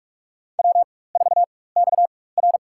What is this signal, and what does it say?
Morse code, heard as a keyed tone, about 46 wpm, spaced as for 21 wpm, 690 Hz. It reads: W4XR
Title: USA Call Signs - 4x Character Spacing - 45wpm